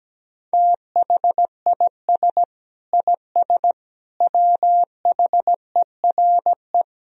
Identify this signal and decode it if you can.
Morse code, heard as a keyed tone, about 17 wpm, 700 Hz. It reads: THIS IS WHERE
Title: Sentences from Top 1000 Words - Review Part 3 - 17wpm